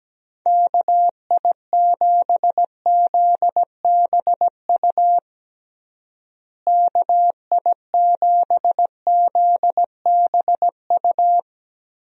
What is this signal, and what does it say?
Morse code, heard as a keyed tone, about 17 wpm, 700 Hz. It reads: KI7ZBU KI7ZBU
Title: USA Call Signs - 6 Characters - Repeated 3x - 17wpm